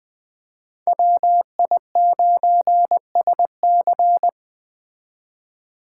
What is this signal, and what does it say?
Morse code, heard as a keyed tone, about 20 wpm, 700 Hz. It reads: WI9SC